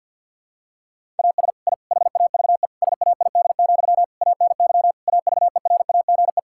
Morse code, heard as a keyed tone, about 50 wpm, 700 Hz. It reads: AS I HAVE SAID, ANY AVERAGE
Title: Sentences from Top 1000 Words - Review Part 6 - 50wpm